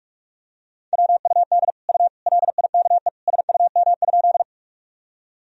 Morse code, heard as a keyed tone, about 46 wpm, 700 Hz. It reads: WUD U LIKE SUM?
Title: Sentences from Top 100 Words and Abbreviations 45wpm